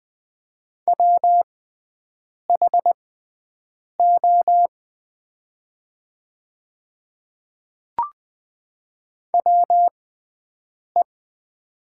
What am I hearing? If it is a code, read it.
Morse code, heard as a keyed tone, about 20 wpm, spaced as for 7 wpm, 700 Hz. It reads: WHO WE
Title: 6x Character Spacing - Top 100 Words - 20wpm